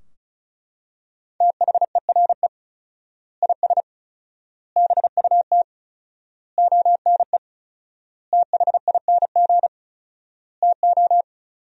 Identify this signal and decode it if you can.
Morse code, heard as a keyed tone, about 36 wpm, 700 Hz. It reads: THERE IS BUT ONE THING TO
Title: 4x Word-Spacing - Sentences from Top 100 Words - 35wpm